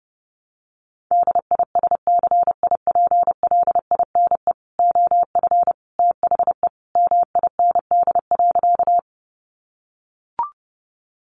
Morse code, heard as a keyed tone, about 28 wpm, 700 Hz. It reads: DISCIPLINE OF THE MIND.
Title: Sentences from Top 900 Words 30wpm